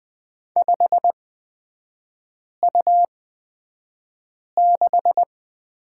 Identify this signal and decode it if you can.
Morse code, heard as a keyed tone, about 20 wpm, spaced as for 5 wpm, 690 Hz. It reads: 5U6
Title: 229 Lesson 37e - Letter x - Three Characters 20wpm